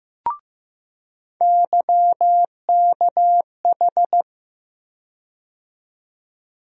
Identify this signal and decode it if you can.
Morse code, heard as a keyed tone, about 15 wpm, 700 Hz. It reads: YKH
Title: USA Call Sign Postfixes - 3 Letters 15wpm